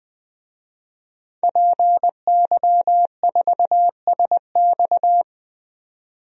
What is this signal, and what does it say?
Morse code, heard as a keyed tone, about 20 wpm, 700 Hz. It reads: PY4SX
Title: DXCC-100 Call Signs - 5 Characters - 20wpm